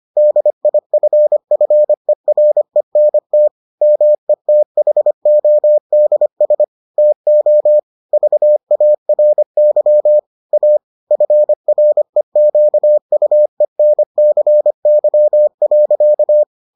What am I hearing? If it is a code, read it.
Morse code, heard as a keyed tone, about 25 wpm, 600 Hz. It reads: DIFFERENT METHODS TO VARY A FREQUENCY.